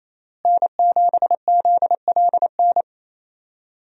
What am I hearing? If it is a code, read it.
Morse code, heard as a keyed tone, about 27 wpm, 700 Hz. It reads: N7ZLN
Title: USA Call Signs - 5 Characters - Repeated 3x - 28wpm